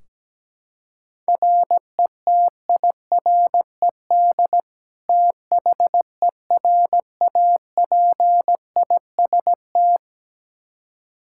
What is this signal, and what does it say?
Morse code, heard as a keyed tone, about 17 wpm, 700 Hz. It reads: RETIRED THERAPIST
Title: Practice QSO - Exchange 3 - More Info - 17wpm